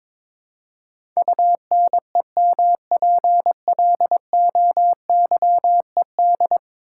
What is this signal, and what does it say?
Morse code, heard as a keyed tone, about 22 wpm, 700 Hz. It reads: UNEMPLOYED
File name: practice / 22wpm / 10 Letter Words - First 3 Letters Given - 22wpm